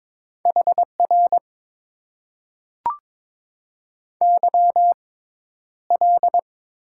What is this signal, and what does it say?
Morse code, heard as a keyed tone, about 22 wpm, spaced as for 11 wpm, 700 Hz. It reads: HR YL
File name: Top 90 QSO Elements - 6x Character Spacing - 22wpm